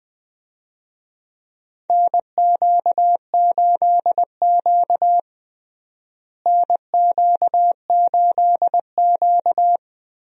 Morse code, heard as a keyed tone, about 20 wpm, 700 Hz. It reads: NQ8Q NQ8Q
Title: USA Call Signs - 4 Characters - Repeated 2x - 20wpm